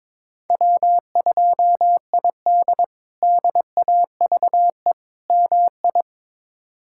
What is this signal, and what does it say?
Morse code, heard as a keyed tone, about 22 wpm, 700 Hz. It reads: W2ID DAVE MI
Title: K1USN SST QSO Practice - 22wpm